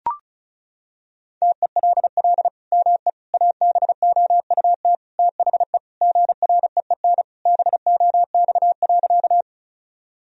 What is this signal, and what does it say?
Morse code, heard as a keyed tone, about 36 wpm, 700 Hz. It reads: TELL ME ABOUT THE GREEN BOX.
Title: Sentences from Top 1000 Words - Review Part 2 - 35wpm